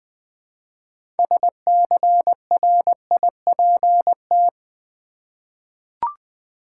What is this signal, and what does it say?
Morse code, heard as a keyed tone, about 20 wpm, 700 Hz. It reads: SCRIPT
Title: Six Letter Words - 20wpm